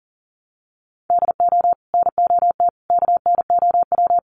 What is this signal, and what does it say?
Morse code, heard as a keyed tone, about 40 wpm, 700 Hz. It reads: DO NOT KNOW
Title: Sets of 5 Words 40wpm